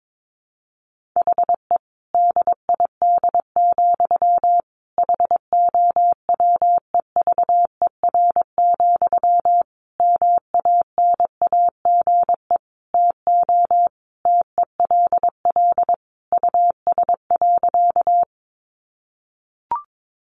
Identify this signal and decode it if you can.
Morse code, heard as a keyed tone, about 22 wpm, 700 Hz. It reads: HE DID, HOWEVER, MANAGE TO TELL US.
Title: Sentences from Top 1400 Words 22wpm